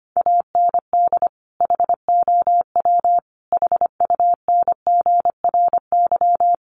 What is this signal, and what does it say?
Morse code, heard as a keyed tone, about 25 wpm, 700 Hz. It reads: AND HOW HUNGRY
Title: Sentences from Top 1400 Words 25wpm